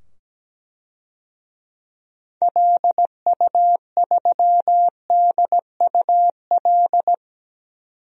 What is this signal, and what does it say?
Morse code, heard as a keyed tone, about 17 wpm, 700 Hz. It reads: LU3DUL
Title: DXCC-100 Call Signs - 6 Characters - 17wpm